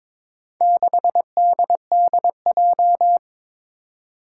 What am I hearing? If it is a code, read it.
Morse code, heard as a keyed tone, about 22 wpm, 700 Hz. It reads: 6DDJ
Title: Four Challenging Letter-Number Combinations 22wpm